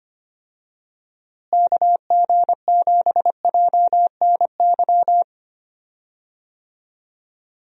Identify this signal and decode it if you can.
Morse code, heard as a keyed tone, about 25 wpm, 700 Hz. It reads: KG7JNY
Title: USA Call Signs - 6 Characters - Repeated 2x - 25wpm